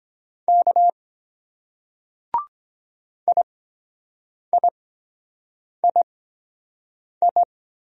Morse code, heard as a keyed tone, about 23 wpm, spaced as for 7 wpm, 700 Hz. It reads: K IIII